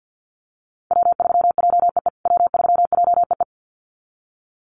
Morse code, heard as a keyed tone, about 50 wpm, 700 Hz. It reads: W3JEE W3JEE
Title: USA Call Signs Repeated 2x 50wpm